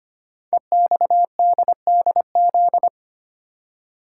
Morse code, heard as a keyed tone, about 25 wpm, 700 Hz. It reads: EXDDZ
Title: Code Groups - 5 Letters - 25wpm